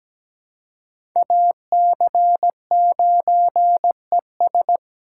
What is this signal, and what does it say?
Morse code, heard as a keyed tone, about 17 wpm, 700 Hz. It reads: AC9ES
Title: USA Call Signs - 5 Characters - 3x Character Spacing - 17wpm